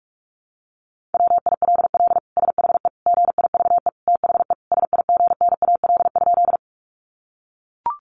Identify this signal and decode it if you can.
Morse code, heard as a keyed tone, about 44 wpm, 700 Hz. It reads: WILL SHE GIVE THE SIGNAL?
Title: Sentences from Top 1400 Words 45wpm